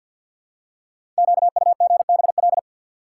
Morse code, heard as a keyed tone, about 48 wpm, 690 Hz. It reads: QUGBL
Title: Code Groups - 5 Letters - 50wpm